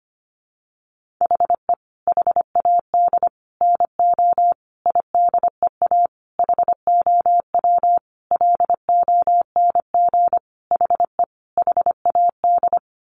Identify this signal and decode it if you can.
Morse code, heard as a keyed tone, about 25 wpm, 700 Hz. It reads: HE HAD NO IDEA HOW LONG HE HAD
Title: Sentences from Top 300 Words 25wpm